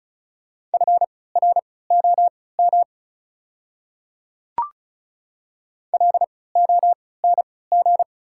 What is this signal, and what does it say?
Morse code, heard as a keyed tone, about 36 wpm, spaced as for 20 wpm, 690 Hz. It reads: FROM LONG
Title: 3x Character Spacing - Top 200 Words - 35wpm